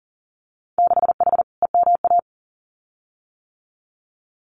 Morse code, heard as a keyed tone, about 40 wpm, 700 Hz. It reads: 6H EMA
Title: Winter Field Day Exchanges Encore 40wpm